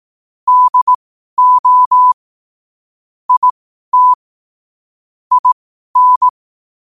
Morse code, heard as a keyed tone, about 18 wpm, spaced as for 12 wpm, 1000 Hz. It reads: DO IT IN
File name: qotdfiles / 12wpm